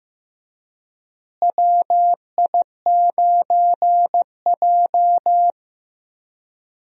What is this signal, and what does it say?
Morse code, heard as a keyed tone, about 15 wpm, 700 Hz. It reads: WI9J
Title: USA Call Signs - 4 Characters - Repeated 3x - 15wpm